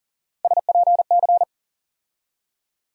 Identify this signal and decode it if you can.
Morse code, heard as a keyed tone, about 40 wpm, 700 Hz. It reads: SPC
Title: USA Call Sign Postfixes - 3 Letters 40wpm